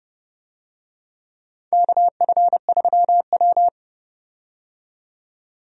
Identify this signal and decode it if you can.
Morse code, heard as a keyed tone, about 32 wpm, 700 Hz. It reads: KF3W
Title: USA Call Signs - 4 Characters - 30wpm